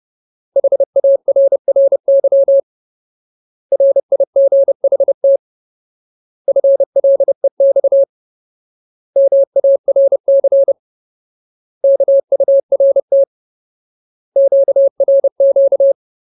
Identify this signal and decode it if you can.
Morse code, heard as a keyed tone, about 30 wpm, 550 Hz. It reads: HARRY RIGHT FLEX MARC KURT QRQ
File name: Ultimate Word List / 30WPM Ultimate